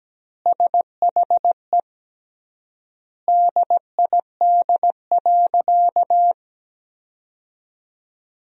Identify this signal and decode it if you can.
Morse code, heard as a keyed tone, about 17 wpm, 700 Hz. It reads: SHE DID.
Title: 3x Word-Spacing - Sentences from Top 100 Words - 17wpm